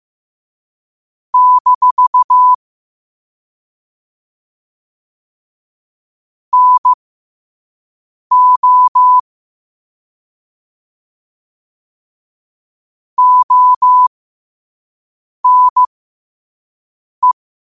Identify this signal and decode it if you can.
Morse code, heard as a keyed tone, about 15 wpm, spaced as for 5 wpm, 1000 Hz. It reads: - NO ONE
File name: qotdfiles / 5wpm